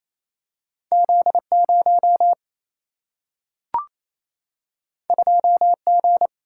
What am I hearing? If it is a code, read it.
Morse code, heard as a keyed tone, about 28 wpm, 700 Hz. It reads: Z 0 2 G
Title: Single-Letter Single-Number 28wpm